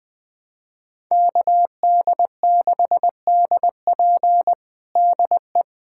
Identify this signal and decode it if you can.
Morse code, heard as a keyed tone, about 20 wpm, 700 Hz. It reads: KD6DP DE
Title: USA Callsign DE Callsign - 5 Characters - 20wpm